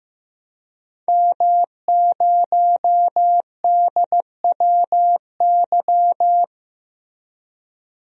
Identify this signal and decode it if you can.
Morse code, heard as a keyed tone, about 15 wpm, 700 Hz. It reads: M0DWY
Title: DXCC-100 Call Signs - 5 Characters - 15wpm